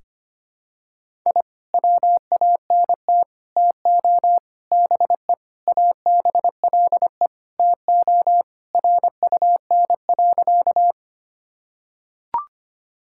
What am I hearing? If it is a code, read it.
Morse code, heard as a keyed tone, about 25 wpm, 700 Hz. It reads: I WANT TO BE ABLE TO RUN.
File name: practice / 25wpm / Sentences with 3-Word Phrases - 25wpm